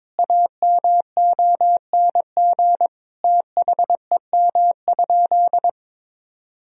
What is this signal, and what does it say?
Morse code, heard as a keyed tone, about 22 wpm, 700 Hz. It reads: AMONG THEM?
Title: Sentences from Top 1000 Words - Review Part 2 - 22wpm